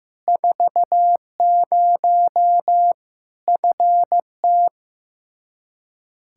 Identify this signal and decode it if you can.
Morse code, heard as a keyed tone, about 15 wpm, 700 Hz. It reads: 40 FT
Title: Practice QSO - Exchange 2 - Info - 15wpm